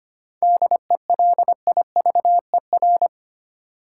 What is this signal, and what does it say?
Morse code, heard as a keyed tone, about 25 wpm, 700 Hz. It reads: DELIVER
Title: Seven Letter Words - 25wpm